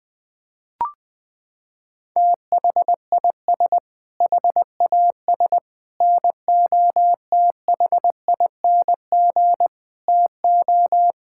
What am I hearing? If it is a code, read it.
Morse code, heard as a keyed tone, about 21 wpm, 700 Hz. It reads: THIS HAS NOTHING TO
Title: Sentences with 5-Word Phrases - 20wpm